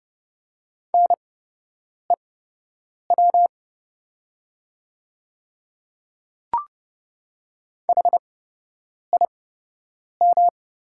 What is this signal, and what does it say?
Morse code, heard as a keyed tone, about 30 wpm, spaced as for 8 wpm, 690 Hz. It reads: NEW HIM
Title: 8x Character Spacing - Top 300 Words - 30wpm